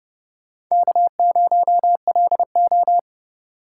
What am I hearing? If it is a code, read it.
Morse code, heard as a keyed tone, about 30 wpm, 700 Hz. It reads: K0LO
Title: USA Call Signs - 2x Character Spacing - 30wpm